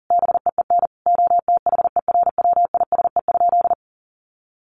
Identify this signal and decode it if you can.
Morse code, heard as a keyed tone, about 40 wpm, 700 Hz. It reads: BEEN OTHERWISE?
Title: Sentences from Top 1200 Words 40wpm